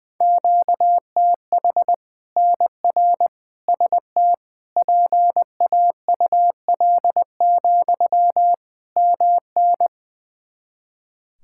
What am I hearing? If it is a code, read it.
Morse code, heard as a keyed tone, about 20 wpm, 700 Hz. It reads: QTH NR ST PAUL, MN
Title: K1USN SST QSO Practice 20wpm